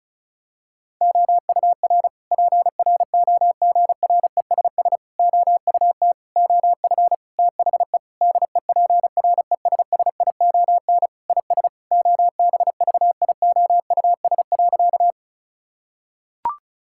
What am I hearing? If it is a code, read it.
Morse code, heard as a keyed tone, about 36 wpm, 700 Hz. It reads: OUR PROGRESS OUT OF THE DEPRESSION IS OBVIOUS.
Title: Sentences from Top 1600 Words 35wpm